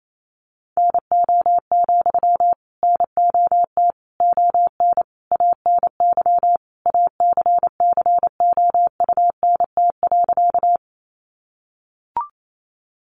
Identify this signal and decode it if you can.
Morse code, heard as a keyed tone, about 28 wpm, 700 Hz. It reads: NO, NOT ON ANY ACCOUNT.
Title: Sentences from Top 900 Words 28wpm